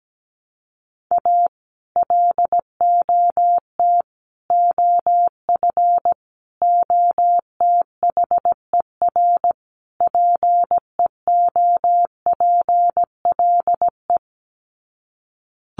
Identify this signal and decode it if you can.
Morse code, heard as a keyed tone, about 17 wpm, 700 Hz. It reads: A LOT OF OTHER PEOPLE
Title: Sets of 5 Words Encore2 17wpm